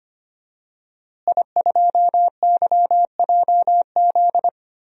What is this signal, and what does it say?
Morse code, heard as a keyed tone, about 25 wpm, 700 Hz. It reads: I2YJZ
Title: DXCC-100 Call Signs - 5 Characters - Repeated 3x - 25wpm